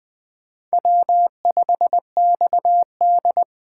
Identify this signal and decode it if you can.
Morse code, heard as a keyed tone, about 24 wpm, 700 Hz. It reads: W5XD
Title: Speed Racing - USA Call Signs 20wpm